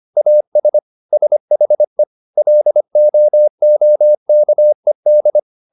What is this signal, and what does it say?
Morse code, heard as a keyed tone, about 25 wpm, 600 Hz. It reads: AS SHE LOOKED